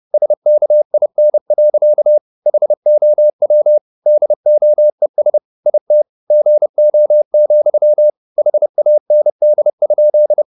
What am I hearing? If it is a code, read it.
Morse code, heard as a keyed tone, about 30 wpm, 600 Hz. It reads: SKIN. HOW DOES IT GO, HAND?
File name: the old man and the sea / the old man and the sea 1 30wpm